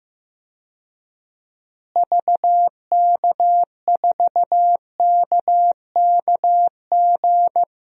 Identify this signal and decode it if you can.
Morse code, heard as a keyed tone, about 15 wpm, 700 Hz. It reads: VK4KKG
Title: DXCC-100 Call Signs - 6 Characters - Repeated 2x - 15wpm